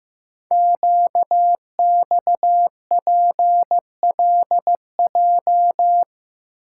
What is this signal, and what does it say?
Morse code, heard as a keyed tone, about 15 wpm, 700 Hz. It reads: QXPLJ
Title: Code Groups - 5 Letters - 15wpm